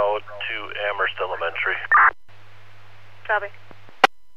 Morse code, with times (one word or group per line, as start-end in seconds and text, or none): none